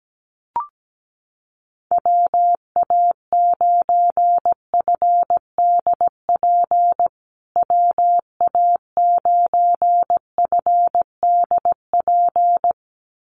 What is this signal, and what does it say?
Morse code, heard as a keyed tone, about 17 wpm, 700 Hz. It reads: WA9FDP WA9FDP
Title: USA Call Signs Repeated 2x 17wpm